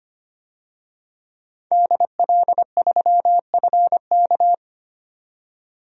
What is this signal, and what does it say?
Morse code, heard as a keyed tone, about 25 wpm, 700 Hz. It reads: DL3FK
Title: DXCC-100 Call Signs - 5 Characters - 25wpm